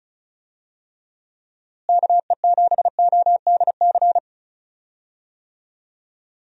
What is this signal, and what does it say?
Morse code, heard as a keyed tone, about 34 wpm, 700 Hz. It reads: KE7ODC
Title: USA Call Signs - 6 Characters - Repeated 3x - 35wpm